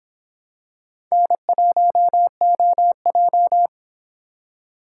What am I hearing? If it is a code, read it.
Morse code, heard as a keyed tone, about 24 wpm, 700 Hz. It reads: N1OJ